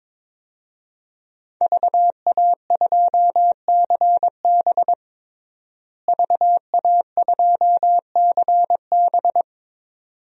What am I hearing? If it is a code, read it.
Morse code, heard as a keyed tone, about 22 wpm, 700 Hz. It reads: VA2CB VA2CB
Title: DXCC-100 Call Signs - 5 Characters - Repeated 2x - 22wpm